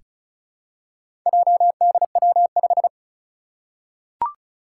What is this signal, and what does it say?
Morse code, heard as a keyed tone, about 36 wpm, 700 Hz. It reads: JDW5
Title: Four Challenging Letter-Number Combinations 35wpm